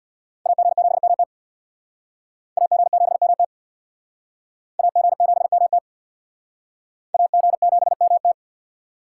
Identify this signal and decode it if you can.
Morse code, heard as a keyed tone, about 62 wpm, 700 Hz. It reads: AG7MT AG7MT AG7MT AG7MT
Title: Speed Racing - USA 2x2 Call Signs 50wpm